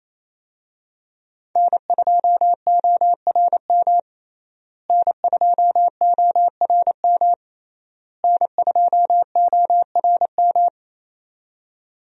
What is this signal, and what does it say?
Morse code, heard as a keyed tone, about 28 wpm, 700 Hz. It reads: N2ORM N2ORM N2ORM
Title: USA Call Signs - 5 Characters - Repeated 3x - 28wpm